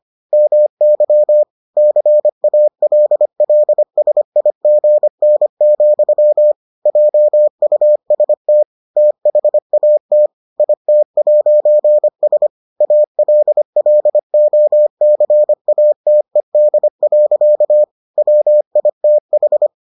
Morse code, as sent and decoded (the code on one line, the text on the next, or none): -- -.-- / -.-. .- .-.. .-.. ... .. --. -. --..-- / .--- ..- ... - / - .... .- - / .. - .----. ... / .- .-.. .-.. --- -.-. .- - . -.. .-.-.- / .-- .. - ....
MY CALLSIGN, JUST THAT IT'S ALLOCATED. WITH